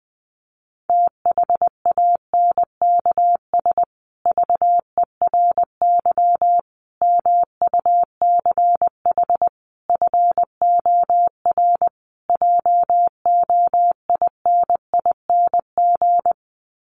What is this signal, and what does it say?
Morse code, heard as a keyed tone, about 20 wpm, 700 Hz. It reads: THANKS VERY MUCH FOR JOINING
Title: Sets of 5 Words Encore2 20wpm